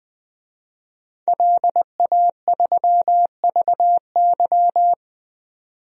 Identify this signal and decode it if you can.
Morse code, heard as a keyed tone, about 20 wpm, 700 Hz. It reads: LA3VY
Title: DXCC-100 Call Signs - 5 Characters - 20wpm